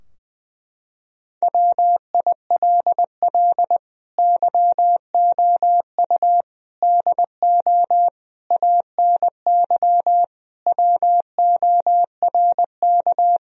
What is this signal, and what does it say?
Morse code, heard as a keyed tone, about 21 wpm, 700 Hz. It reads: WILL YOU DO ANY WORK
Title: Sentences from Top 1000 Words - Review Part 3 - 20wpm